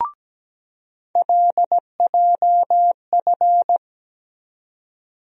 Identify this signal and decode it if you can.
Morse code, heard as a keyed tone, about 17 wpm, 700 Hz. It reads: LJF
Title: USA Call Sign Postfixes - 3 Letters 17wpm